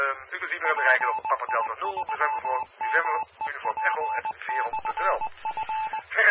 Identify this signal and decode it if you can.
Morse code, heard as a keyed tone, about 20 wpm, 870 Hz, with speech and radio noise. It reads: PI3MEP/F